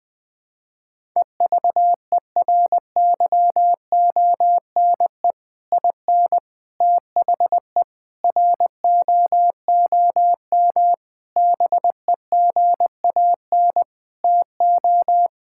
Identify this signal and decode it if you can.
Morse code, heard as a keyed tone, about 21 wpm, 700 Hz. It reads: EVERYONE IN THE ROOM BEGAN TO
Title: Sentences from Top 1000 Words - Review Part 5 - 20wpm